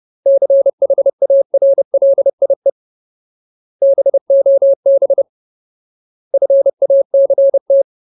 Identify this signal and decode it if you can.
Morse code, heard as a keyed tone, about 30 wpm, 550 Hz. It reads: CHARLIE BOB FACT
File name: Ultimate Word List / 30WPM Ultimate